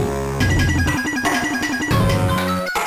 There is no code.